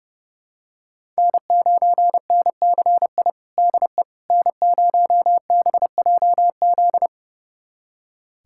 Morse code, heard as a keyed tone, about 30 wpm, 700 Hz. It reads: N9NCI DE N0BJZ
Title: USA Callsign DE Callsign - 5 Characters - 30wpm